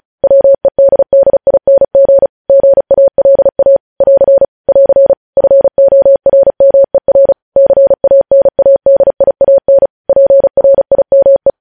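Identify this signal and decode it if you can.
Morse code, heard as a keyed tone, about 36 wpm, 550 Hz. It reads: WEDDING GALA <AR> <AR> FORMER CANADIAN PRIME
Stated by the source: Morse code practice sending news headlines at 35 wpm.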